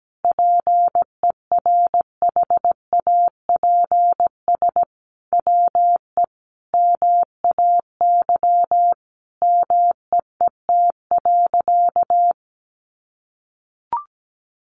Morse code, heard as a keyed tone, about 17 wpm, 700 Hz. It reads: PERHAPS WE MAY MEET.
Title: Sentences from Top 1100 Words 17wpm